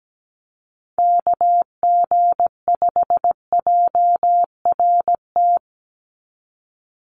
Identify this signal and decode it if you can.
Morse code, heard as a keyed tone, about 17 wpm, 700 Hz. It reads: KG5JRT
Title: USA Call Signs 17wpm